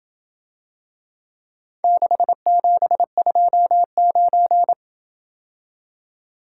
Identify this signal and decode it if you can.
Morse code, heard as a keyed tone, about 27 wpm, 700 Hz. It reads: 6729